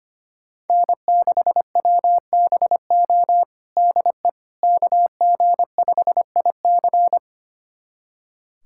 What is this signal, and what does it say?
Morse code, heard as a keyed tone, about 25 wpm, 700 Hz. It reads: N6WBO DE KG5IC
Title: USA Callsign DE Callsign - 5 Characters - 25wpm